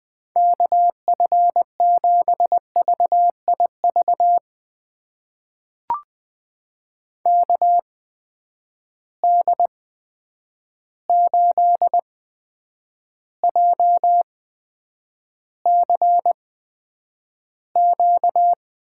Morse code, heard as a keyed tone, about 20 wpm, spaced as for 9 wpm, 700 Hz. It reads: KF7VIV KD8JCQ
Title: USA Call Signs - 6 Characters - 8x Character Spacing - 20wpm